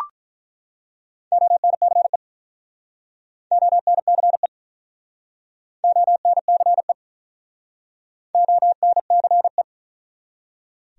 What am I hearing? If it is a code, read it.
Morse code, heard as a keyed tone, about 47 wpm, 700 Hz. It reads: ONCE ONCE ONCE ONCE